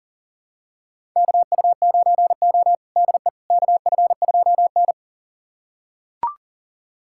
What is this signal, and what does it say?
Morse code, heard as a keyed tone, about 40 wpm, 700 Hz. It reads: KU9O DE KF2N
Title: USA Callsign DE Callsign - 4 Characters - 40wpm